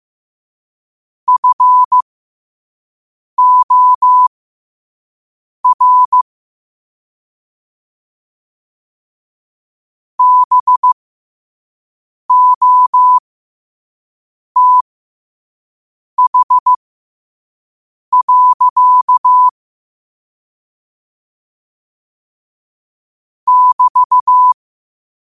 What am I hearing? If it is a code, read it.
Morse code, heard as a keyed tone, about 15 wpm, spaced as for 5 wpm, 1000 Hz. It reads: FOR BOTH. <BT>
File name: qotdfiles / 5wpm